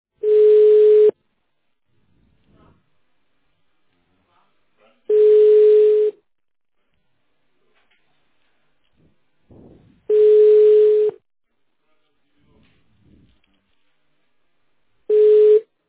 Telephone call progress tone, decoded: ringback tone

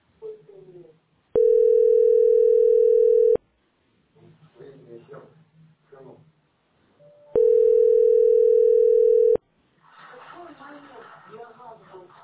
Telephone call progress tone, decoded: ringback tone

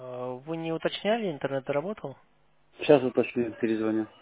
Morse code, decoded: none

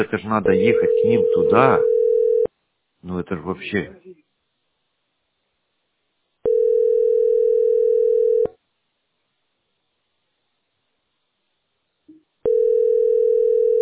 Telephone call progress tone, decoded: ringback tone

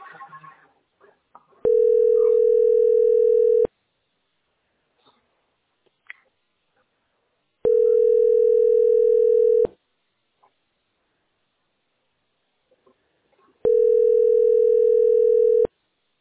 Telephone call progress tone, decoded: ringback tone

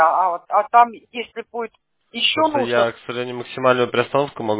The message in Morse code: none

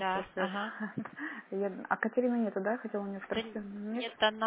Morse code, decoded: none

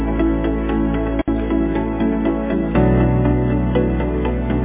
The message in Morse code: none

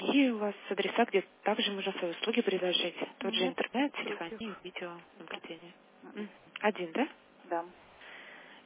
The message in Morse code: none

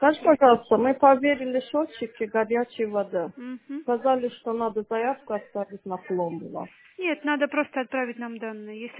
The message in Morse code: none

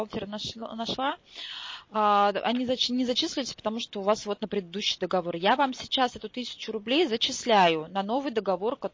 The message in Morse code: none

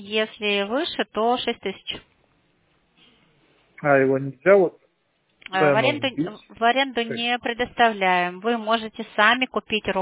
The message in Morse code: none

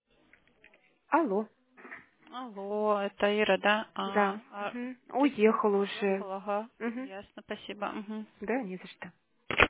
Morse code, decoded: none

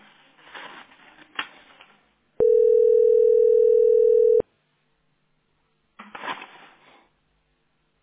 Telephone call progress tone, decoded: ringback tone

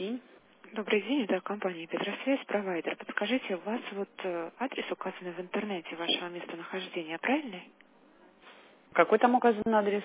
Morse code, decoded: none

